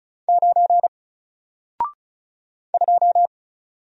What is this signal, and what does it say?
Morse code, heard as a keyed tone, about 36 wpm, 700 Hz. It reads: 9 2
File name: Single-Letter-Number-35wpm